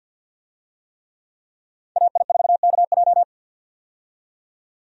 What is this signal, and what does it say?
Morse code, heard as a keyed tone, about 50 wpm, 700 Hz. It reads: AI4KJ
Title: USA Call Signs - 5 Characters - 50wpm